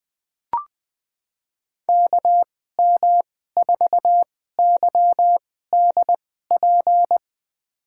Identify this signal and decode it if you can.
Morse code, heard as a keyed tone, about 20 wpm, spaced as for 15 wpm, 700 Hz. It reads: KM4YDP
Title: USA Call Signs - 6 Characters - 2x Character Spacing - 20wpm